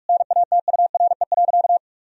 Morse code, heard as a keyed tone, about 41 wpm, 700 Hz. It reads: NATURE.
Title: Sentences from Top 1000 Words - Review Part 3 - 45wpm